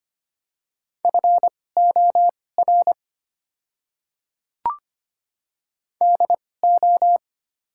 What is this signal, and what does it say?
Morse code, heard as a keyed tone, about 25 wpm, spaced as for 18 wpm, 700 Hz. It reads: FOR DO
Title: 2x Character Spacing - Top 200 Words - 25wpm